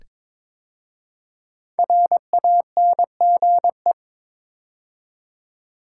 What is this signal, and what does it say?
Morse code, heard as a keyed tone, about 22 wpm, 700 Hz. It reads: RANGE
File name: 5 Letter Words - First 3 Letters Given - 22wpm